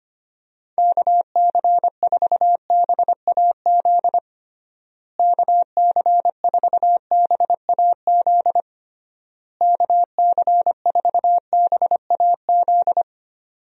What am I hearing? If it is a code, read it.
Morse code, heard as a keyed tone, about 25 wpm, 700 Hz. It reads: KC4BAZ KC4BAZ KC4BAZ
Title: USA Call Signs - 6 Characters - Repeated 3x - 25wpm